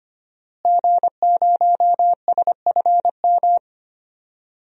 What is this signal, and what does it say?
Morse code, heard as a keyed tone, about 25 wpm, 700 Hz. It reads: G0SFM
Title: DXCC-100 Call Signs - 5 Characters - Repeated 2x - 25wpm